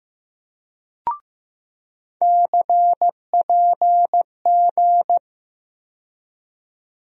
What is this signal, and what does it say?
Morse code, heard as a keyed tone, about 15 wpm, 700 Hz. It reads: CPG
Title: USA Call Sign Postfixes - 3 Letters 15wpm